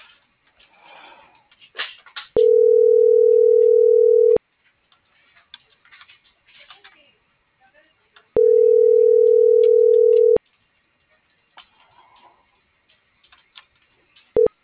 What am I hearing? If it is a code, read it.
Ringback tone (North America), 440 + 480 Hz.
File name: CENTRAL MABE / out edwinomarojeda